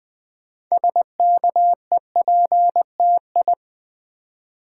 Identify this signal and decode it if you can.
Morse code, heard as a keyed tone, about 20 wpm, 700 Hz. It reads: SKEPTI
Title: Build A Word - 3 Letters At A Time - 9-Letter Words - 20wpm